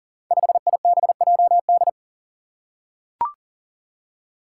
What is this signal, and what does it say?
Morse code, heard as a keyed tone, about 38 wpm, 700 Hz. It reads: 5IBJD
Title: Five Challenging Letter-Number Combinations 40wpm